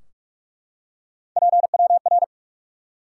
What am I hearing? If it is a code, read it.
Morse code, heard as a keyed tone, about 44 wpm, 700 Hz. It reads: PWR
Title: Top 90 QSO Elements 45wpm